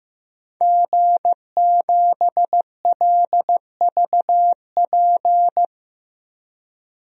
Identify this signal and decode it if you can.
Morse code, heard as a keyed tone, about 15 wpm, 700 Hz. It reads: G7LVP
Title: DXCC-100 Call Signs - 5 Characters - Repeated 2x - 15wpm